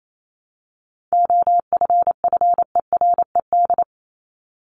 Morse code, heard as a keyed tone, about 28 wpm, 700 Hz. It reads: OFFERED
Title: ED Suffix - Top 100 Words with Base Words 28wpm